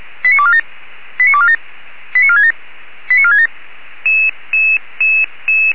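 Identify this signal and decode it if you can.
Five-tone selective call (ZVEI), heard as quick runs of five tones, tones of about 70 ms. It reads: 89247 89247 89467 89467
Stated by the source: ZVEI five-tone call 89305 received by SDR.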